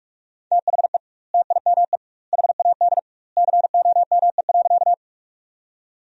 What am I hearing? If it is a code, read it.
Morse code, heard as a keyed tone, about 47 wpm, 700 Hz. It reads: THE TIME HAD COME.
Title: 2x Word-Spacing - Sentences from Top 100 Words - 45wpm